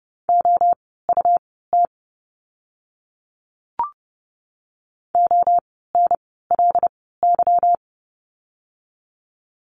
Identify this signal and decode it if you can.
Morse code, heard as a keyed tone, about 30 wpm, spaced as for 17 wpm, 700 Hz. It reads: OUT ONLY
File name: course / 30:10wpm / 237 Lesson 38f - Letter q - Words 30wpm